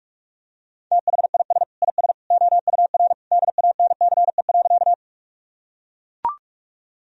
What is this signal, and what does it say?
Morse code, heard as a keyed tone, about 41 wpm, 700 Hz. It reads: THIS IS OUR DANCE.
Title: Sentences from Top 1000 Words - Review Part 4 - 45wpm